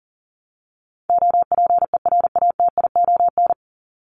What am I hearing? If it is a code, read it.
Morse code, heard as a keyed tone, about 40 wpm, 700 Hz. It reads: OPERATION
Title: Top 1501-1750 Words 40wpm